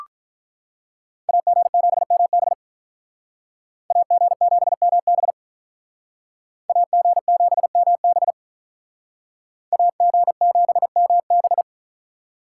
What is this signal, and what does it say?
Morse code, heard as a keyed tone, about 44 wpm, 700 Hz. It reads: AG7MB AG7MB AG7MB AG7MB